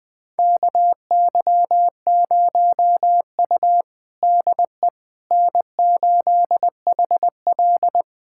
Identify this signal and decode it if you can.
Morse code, heard as a keyed tone, about 20 wpm, 700 Hz. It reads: KY0U DE N8HL
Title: USA Callsign DE Callsign - 4 Characters - 20wpm